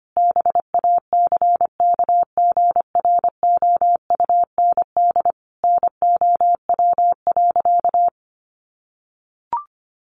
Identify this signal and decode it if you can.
Morse code, heard as a keyed tone, about 25 wpm, 700 Hz. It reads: BACKGROUND NOW.
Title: Sentences from Top 1300 Words 25wpm